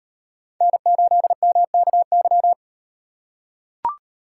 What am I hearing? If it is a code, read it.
Morse code, heard as a keyed tone, about 35 wpm, 700 Hz. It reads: N8MKY